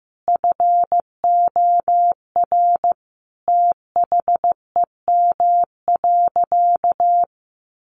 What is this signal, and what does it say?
Morse code, heard as a keyed tone, about 14 wpm, 700 Hz. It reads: FOR THEM.
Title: Sentences from Top 900 Words 15wpm